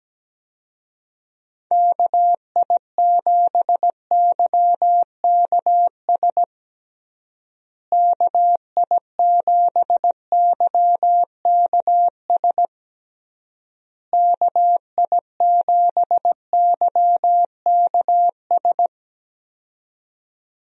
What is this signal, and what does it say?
Morse code, heard as a keyed tone, about 17 wpm, 700 Hz. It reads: KI7YKS KI7YKS KI7YKS
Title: USA Call Signs - 6 Characters - Repeated 3x - 17wpm